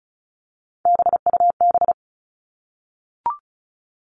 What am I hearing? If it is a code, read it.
Morse code, heard as a keyed tone, about 34 wpm, 700 Hz. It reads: BUB
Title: All Cut Numbers - 3-Digit 35wpm